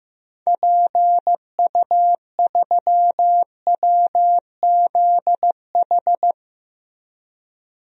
Morse code, heard as a keyed tone, about 15 wpm, 700 Hz. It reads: PU3WZH
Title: DXCC-100 Call Signs - 6 Characters - Repeated 2x - 15wpm